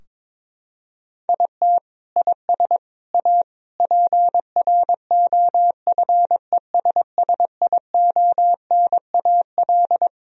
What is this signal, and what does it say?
Morse code, heard as a keyed tone, about 22 wpm, 700 Hz. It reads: IT IS A PROFESSIONAL
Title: Sentences from Top 1000 Words - Review Part 4 - 22wpm